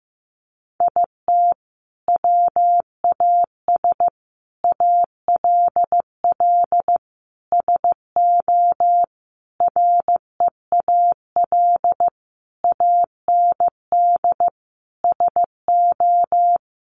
Morse code, heard as a keyed tone, about 15 wpm, 700 Hz. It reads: IT WAS ALL SO REAL AND SO
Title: Sentences from Top 1000 Words 15wpm